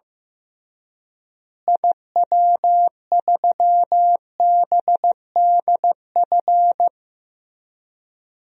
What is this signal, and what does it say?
Morse code, heard as a keyed tone, about 15 wpm, 700 Hz. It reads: IW3BDF
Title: DXCC-100 Call Signs - 6 Characters - Repeated 3x - 15wpm